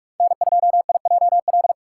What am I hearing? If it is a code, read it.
Morse code, heard as a keyed tone, about 46 wpm, 700 Hz. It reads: N2IJL
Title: USA Call Signs - 6x Character Spacing - 45wpm